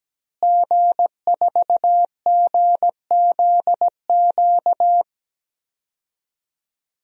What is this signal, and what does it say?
Morse code, heard as a keyed tone, about 17 wpm, 700 Hz. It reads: G4GZQ
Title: DXCC-100 Call Signs - 5 Characters - Repeated 3x - 17wpm